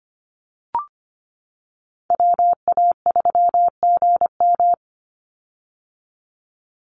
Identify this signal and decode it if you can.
Morse code, heard as a keyed tone, about 25 wpm, 700 Hz. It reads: WA3GM
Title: Call Signs - 25wpm